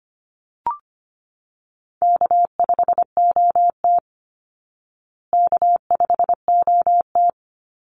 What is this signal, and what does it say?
Morse code, heard as a keyed tone, about 25 wpm, 700 Hz. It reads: K5OT K5OT
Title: Easy Call Signs Repeated 2x - 25wpm